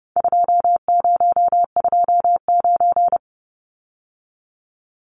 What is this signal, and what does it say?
Morse code, heard as a keyed tone, about 30 wpm, 700 Hz. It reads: 2029
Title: Numbers - 4-Digit 30wpm